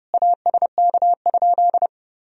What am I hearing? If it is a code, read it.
Morse code, heard as a keyed tone, about 30 wpm, 700 Hz. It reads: ASK?
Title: Sentences from Top 200 Words and Abbreviations 30wpm